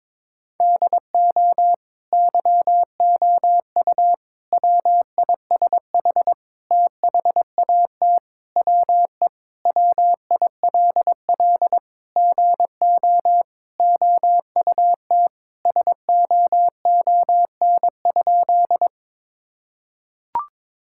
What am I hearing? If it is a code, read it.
Morse code, heard as a keyed tone, about 22 wpm, 700 Hz. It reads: DO YOU WISH THAT WE WILL GO OUT SOON?
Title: Sentences from Top 1600 Words 22wpm